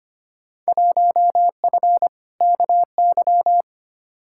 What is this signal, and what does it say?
Morse code, heard as a keyed tone, about 25 wpm, 700 Hz. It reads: 1F KY
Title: Field Day Exchanges Encore - 25wpm